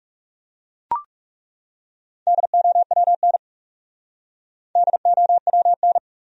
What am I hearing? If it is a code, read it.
Morse code, heard as a keyed tone, about 40 wpm, 700 Hz. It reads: DOWN DOWN